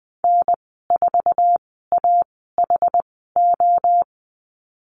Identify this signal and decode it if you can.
Morse code, heard as a keyed tone, about 20 wpm, spaced as for 14 wpm, 700 Hz. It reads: N4AHO
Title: 069 Lesson 12g - Letter d - Call Signs 20wpm